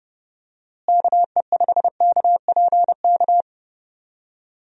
Morse code, heard as a keyed tone, about 31 wpm, 700 Hz. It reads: KE5KPK
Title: USA Call Signs - 6 Characters - Repeated 2x - 30wpm